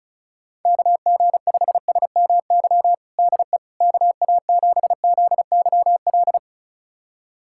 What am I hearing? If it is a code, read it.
Morse code, heard as a keyed tone, about 36 wpm, 700 Hz. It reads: KG5SMY DE KA7ZYL
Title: USA Callsign DE Callsign - 6 Characters - 35wpm